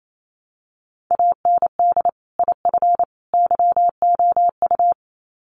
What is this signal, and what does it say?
Morse code, heard as a keyed tone, about 28 wpm, 700 Hz. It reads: AND IF YOU
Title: Sets of 3 Words 28wpm